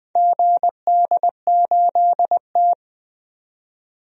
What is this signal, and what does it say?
Morse code, heard as a keyed tone, about 20 wpm, 700 Hz. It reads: GD8T
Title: DXCC-100 Call Signs - 4 Characters - 20wpm